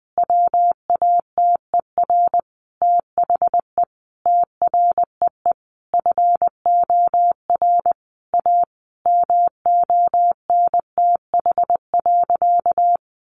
Morse code, heard as a keyed tone, about 20 wpm, 700 Hz. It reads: WATER THE TREE FOR A MONTH.
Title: Easy Phrases 20wpm